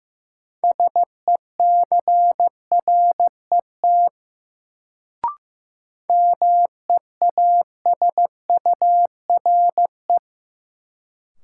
Morse code, heard as a keyed tone, about 15 wpm, 700 Hz. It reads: SECRET MEASURE
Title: Sentences from Top 1200 Words - New Word Drill Alt - 15wpm